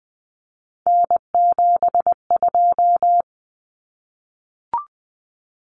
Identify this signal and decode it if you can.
Morse code, heard as a keyed tone, about 20 wpm, 700 Hz. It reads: N72
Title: Common Cut Numbers - 3-Digit 20wpm